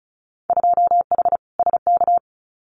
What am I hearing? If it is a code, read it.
Morse code, heard as a keyed tone, about 34 wpm, 700 Hz. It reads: 2H SK
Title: Winter Field Day Exchanges Encore 35wpm